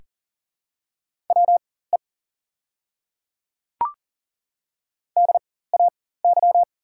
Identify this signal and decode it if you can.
Morse code, heard as a keyed tone, about 40 wpm, spaced as for 19 wpm, 690 Hz. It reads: WE DAY
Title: 4x Character Spacing - Top 100 Words - 40wpm